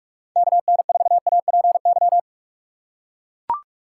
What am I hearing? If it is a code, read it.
Morse code, heard as a keyed tone, about 43 wpm, 700 Hz. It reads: KN4APY
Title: USA Call Signs - 6 Characters - 6x Character Spacing - 45wpm